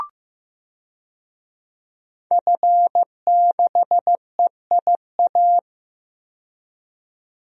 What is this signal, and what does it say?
Morse code, heard as a keyed tone, about 15 wpm, 700 Hz. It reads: F6EIA
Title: DXCC-100 Call Signs - 5 Characters - 15wpm